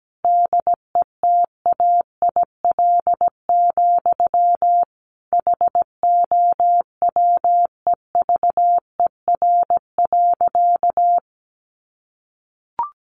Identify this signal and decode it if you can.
Morse code, heard as a keyed tone, about 17 wpm, 700 Hz. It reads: DETAIL, HOWEVER.
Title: Sentences from Top 1500 Words 17wpm